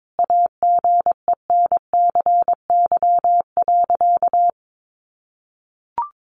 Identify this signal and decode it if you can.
Morse code, heard as a keyed tone, about 22 wpm, 700 Hz. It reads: AGENCY.
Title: Sentences from Top 1300 Words 22wpm